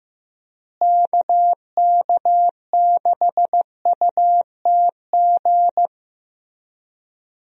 Morse code, heard as a keyed tone, about 15 wpm, 700 Hz. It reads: KK6UTG
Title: USA Call Signs - 6 Characters - Repeated 3x - 15wpm